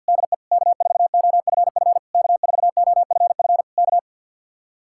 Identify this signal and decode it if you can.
Morse code, heard as a keyed tone, about 67 wpm, spaced as for 58 wpm, 700 Hz. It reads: DE K4OFF K4OFF K
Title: Speed-Racing Common QSO Phrases - 50wpm